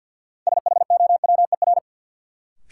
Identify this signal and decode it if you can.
Morse code, heard as a keyed tone, about 53 wpm, 700 Hz. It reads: SHOWER
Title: Sentences from Top 1600 Words - New Word Drill Alt - 50wpm